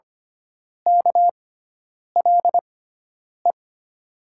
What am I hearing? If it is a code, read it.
Morse code, heard as a keyed tone, about 25 wpm, spaced as for 9 wpm, 700 Hz. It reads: KLE
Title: USA Call Signs - 6 Characters - 6x Character Spacing - 25wpm